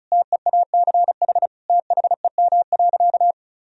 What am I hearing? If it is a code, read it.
Morse code, heard as a keyed tone, about 36 wpm, 700 Hz. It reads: TEACH THEM.
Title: Sentences from Top 1000 Words - Review Part 2 - 35wpm